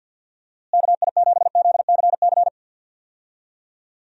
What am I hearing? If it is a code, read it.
Morse code, heard as a keyed tone, about 53 wpm, 700 Hz. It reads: KI7ZCC
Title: USA Call Signs - 6 Characters - Repeated 2x - 50wpm